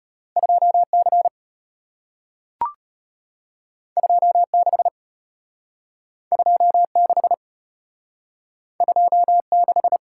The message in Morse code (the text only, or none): 2C 26 26 26